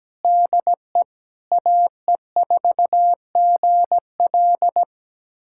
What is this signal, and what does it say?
Morse code, heard as a keyed tone, about 17 wpm, 700 Hz. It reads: DE AE4GL
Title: USA Callsign DE Callsign - 5 Characters - 17wpm